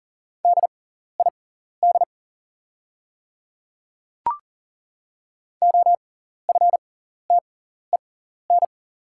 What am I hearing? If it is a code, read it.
Morse code, heard as a keyed tone, about 41 wpm, spaced as for 14 wpm, 690 Hz. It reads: DID OFTEN
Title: 6x Character Spacing - Top 300 Words - 40wpm